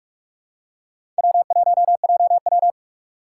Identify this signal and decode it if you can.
Morse code, heard as a keyed tone, about 42 wpm, 700 Hz. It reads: W1JW
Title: USA Call Signs - 4 Characters - 4x Character Spacing - 45wpm